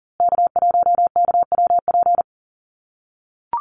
Morse code, heard as a keyed tone, about 40 wpm, 700 Hz. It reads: K1KWP
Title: USA Call Signs Repeated 3x 40wpm